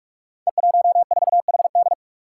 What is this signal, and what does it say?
Morse code, heard as a keyed tone, about 44 wpm, 700 Hz. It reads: E14HD
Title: Five Challenging Letter-Number Combinations 45wpm